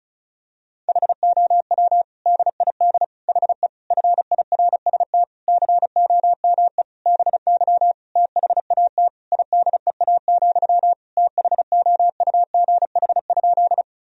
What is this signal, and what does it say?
Morse code, heard as a keyed tone, about 36 wpm, 700 Hz. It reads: HOW DID HE FIRST COME BY THAT IDEA, THOUGH?
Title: Sentences from Top 1000 Words - Review Part 2 - 35wpm